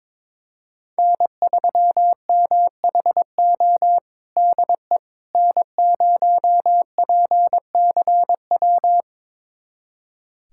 Morse code, heard as a keyed tone, about 22 wpm, 700 Hz. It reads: N3MHO DE N0PCW
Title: USA Callsign DE Callsign - 5 Characters - 22wpm